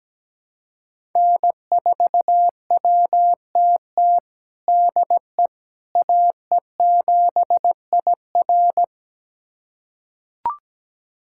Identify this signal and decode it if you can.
Morse code, heard as a keyed tone, about 17 wpm, 700 Hz. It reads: N4WTT DE AE7IR
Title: USA Callsign DE Callsign - 5 Characters - 17wpm